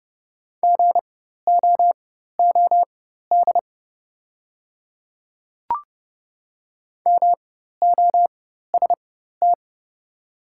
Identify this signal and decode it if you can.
Morse code, heard as a keyed tone, about 30 wpm, spaced as for 14 wpm, 700 Hz. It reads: GOOD MOST
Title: 4x Character Spacing - Top 200 Words - 30wpm